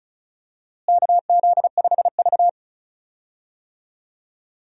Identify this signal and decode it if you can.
Morse code, heard as a keyed tone, about 36 wpm, 700 Hz. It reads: KZ5V
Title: USA Call Signs - 4 Characters - Repeated 3x - 35wpm